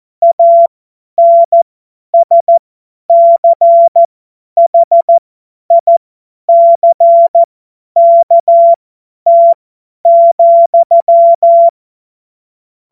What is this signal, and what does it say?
Morse code, heard as a keyed tone, about 14 wpm, spaced as for 10 wpm, 680 Hz. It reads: ANSCHICKT,
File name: Morsekurs / Witze / Witz 15. Februar 2022 50 F70